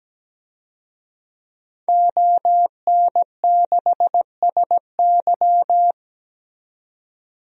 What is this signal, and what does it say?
Morse code, heard as a keyed tone, about 17 wpm, 700 Hz. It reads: ON6SY
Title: DXCC-100 Call Signs - 5 Characters - 17wpm